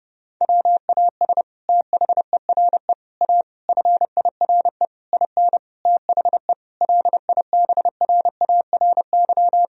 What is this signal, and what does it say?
Morse code, heard as a keyed tone, about 30 wpm, 700 Hz. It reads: WAS THERE A FIRE IN THE LIBRARY
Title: Sentences from Top 1000 Words - Review Part 4 - 30wpm